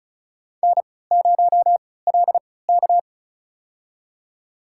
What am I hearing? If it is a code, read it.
Morse code, heard as a keyed tone, about 36 wpm, spaced as for 20 wpm, 690 Hz. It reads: N0LK
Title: USA Call Signs - 4 Characters - 3x Character Spacing - 35wpm